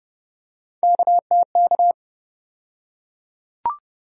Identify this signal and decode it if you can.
Morse code, heard as a keyed tone, about 32 wpm, 700 Hz. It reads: KTK
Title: USA Call Sign Postfixes - 3 Letters 30wpm